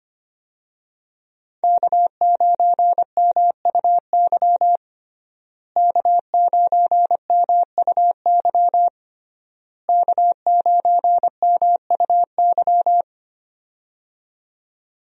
Morse code, heard as a keyed tone, about 25 wpm, 700 Hz. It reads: K9MUY K9MUY K9MUY
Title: USA Call Signs - 5 Characters - Repeated 3x - 25wpm